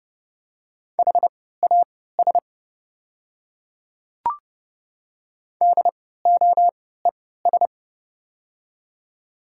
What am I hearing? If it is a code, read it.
Morse code, heard as a keyed tone, about 30 wpm, spaced as for 17 wpm, 700 Hz. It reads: HAS DOES
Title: 3x Character Spacing - Top 200 Words - 30wpm